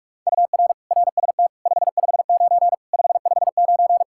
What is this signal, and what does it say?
Morse code, heard as a keyed tone, about 46 wpm, 700 Hz. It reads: UR RST 559 559
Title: Practice QSO - Exchange 1 - Beginning - 45wpm